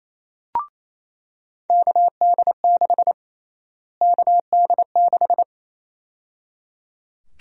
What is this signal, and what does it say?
Morse code, heard as a keyed tone, about 28 wpm, 700 Hz. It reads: KD6 KD6
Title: USA Call Sign Prefixes - Repeated 2x - 28wpm